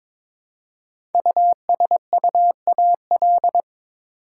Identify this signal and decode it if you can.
Morse code, heard as a keyed tone, about 22 wpm, 700 Hz. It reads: USUAL
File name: 5 Letter Words - First 3 Letters Given - 22wpm